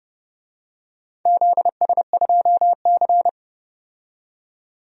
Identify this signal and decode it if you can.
Morse code, heard as a keyed tone, about 32 wpm, 700 Hz. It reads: ZS2C
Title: DXCC-100 Call Signs - 4 Characters - 30wpm